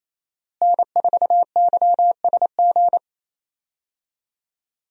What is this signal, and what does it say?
Morse code, heard as a keyed tone, about 27 wpm, 700 Hz. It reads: N4YSG
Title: USA Call Signs - 5 Characters - Repeated 3x - 28wpm